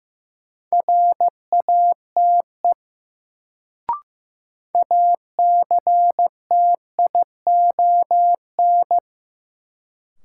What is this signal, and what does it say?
Morse code, heard as a keyed tone, about 15 wpm, 700 Hz. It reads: RATE ACTION
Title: Sentences from Top 700 Words - New Word Drill - 15wpm